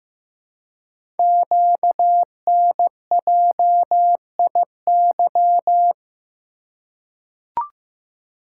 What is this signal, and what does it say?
Morse code, heard as a keyed tone, about 15 wpm, 700 Hz. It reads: QNJIY
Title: Code Groups - 5 Letters - 15wpm